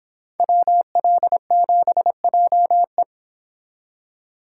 Morse code, heard as a keyed tone, about 29 wpm, 700 Hz. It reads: WL7JE